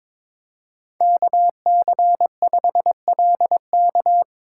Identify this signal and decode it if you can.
Morse code, heard as a keyed tone, about 22 wpm, 700 Hz. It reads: KC5LK
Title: USA Call Signs - 5 Characters - 4x Character Spacing - 22wpm